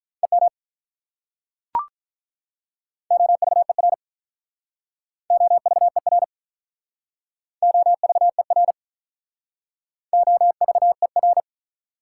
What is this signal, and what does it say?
Morse code, heard as a keyed tone, about 43 wpm, 700 Hz. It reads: EM OVER OVER OVER OVER
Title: Speed-Racing Prefixes and Suffixes 35wpm